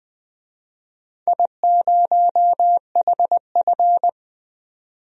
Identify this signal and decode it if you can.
Morse code, heard as a keyed tone, about 21 wpm, 700 Hz. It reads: I0HF
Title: DXCC-100 Call Signs - 4 Characters - 20wpm